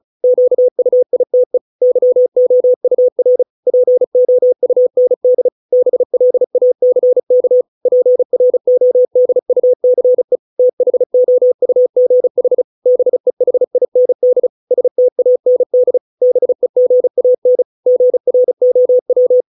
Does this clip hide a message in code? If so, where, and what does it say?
Yes: Morse code from 0.2 to 19.4 s, heard as a keyed tone, about 35 wpm, 500 Hz. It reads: QUITE YOUR POUND BLACK PRODUCE THOUGH BEHIND STAND BEGAN GROW